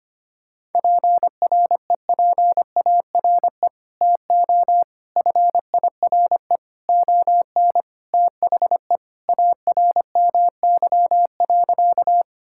Morse code, heard as a keyed tone, about 25 wpm, 700 Hz. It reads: PREPARE TO FIRE ON THE ARMY.
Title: Sentences from Top 1000 Words - Review Part 6 - 25wpm